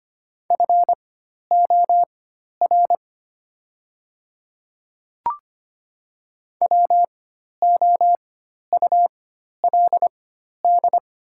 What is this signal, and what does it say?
Morse code, heard as a keyed tone, about 25 wpm, spaced as for 12 wpm, 700 Hz. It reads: FOR WOULD
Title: 4x Character Spacing - Top 100 Words - 25wpm